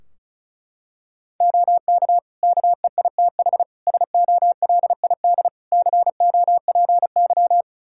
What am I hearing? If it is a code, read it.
Morse code, heard as a keyed tone, about 36 wpm, 700 Hz. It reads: OK KEITH SOLID COPY
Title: Practice QSO - Exchange 3 - More Info - 35wpm